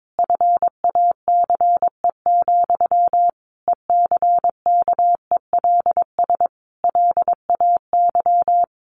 Morse code, heard as a keyed tone, about 22 wpm, 700 Hz. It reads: FACE, ECKELS LAY
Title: A Sound Of Thunder 22wpm